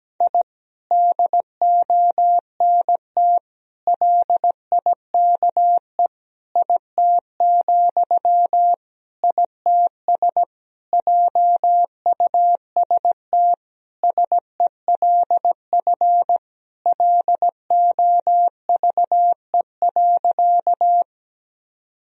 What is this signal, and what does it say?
Morse code, heard as a keyed tone, about 17 wpm, 700 Hz. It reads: I DONT LIKE IT, ITS JUST SELF LOVE.
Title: Sentences from Top 1000 Words - Review Part 1 - 17wpm